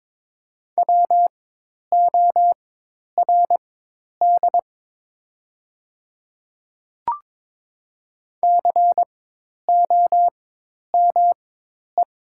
Morse code, heard as a keyed tone, about 22 wpm, spaced as for 10 wpm, 700 Hz. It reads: WORD COME
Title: 4x Character Spacing - Top 100 Words - 22wpm